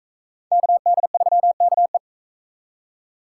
Morse code, heard as a keyed tone, about 40 wpm, 700 Hz. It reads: KD3KE